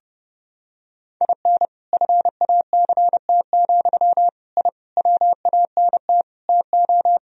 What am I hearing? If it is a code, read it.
Morse code, heard as a keyed tone, about 30 wpm, 700 Hz. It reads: IN FACT, I WANT TO
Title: Sentences from Top 1000 Words - Review Part 3 - 30wpm